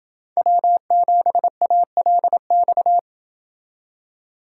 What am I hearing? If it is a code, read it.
Morse code, heard as a keyed tone, about 29 wpm, 700 Hz. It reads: W7ALX